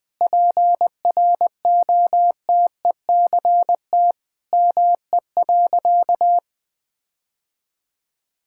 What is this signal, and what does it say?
Morse code, heard as a keyed tone, about 21 wpm, 700 Hz. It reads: PROTECT ME.
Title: Sentences from Top 1600 Words 20wpm